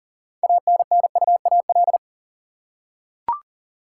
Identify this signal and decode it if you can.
Morse code, heard as a keyed tone, about 40 wpm, 700 Hz. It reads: ANNUAL
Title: Six Letter Words - 40wpm